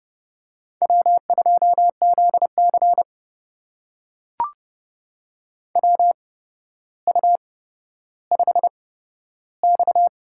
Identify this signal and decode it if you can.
Morse code, heard as a keyed tone, about 31 wpm, spaced as for 13 wpm, 700 Hz. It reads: W2ZC WU5X
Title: USA Call Signs - 4 Characters - 8x Character Spacing - 30wpm